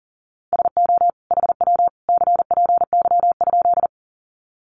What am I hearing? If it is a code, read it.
Morse code, heard as a keyed tone, about 40 wpm, 700 Hz. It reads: SO HW CPY?
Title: Common QSO Phrases 40wpm